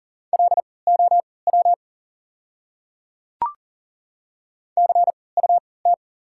Morse code, heard as a keyed tone, about 40 wpm, spaced as for 23 wpm, 700 Hz. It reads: LOW CUT
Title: 3x Character Spacing - Top 300 Words - 40wpm